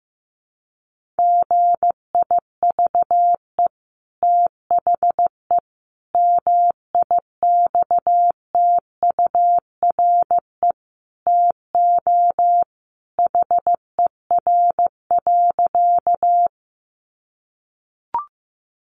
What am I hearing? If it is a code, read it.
Morse code, heard as a keyed tone, about 15 wpm, 700 Hz. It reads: GIVE THE MIXTURE TO HER.
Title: Sentences from Top 1500 Words 15wpm